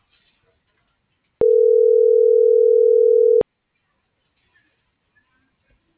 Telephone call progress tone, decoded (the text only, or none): ringback tone